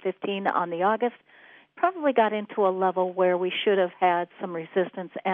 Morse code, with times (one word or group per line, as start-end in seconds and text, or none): none